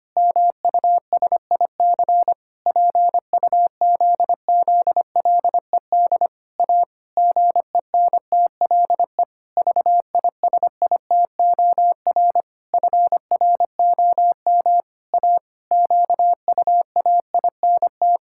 Morse code, heard as a keyed tone, about 25 wpm, 700 Hz. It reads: MUSIC PUZZLED A GENTLE VISITOR FROM A QUAINT
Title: Pangram Challenge v2 - 25wpm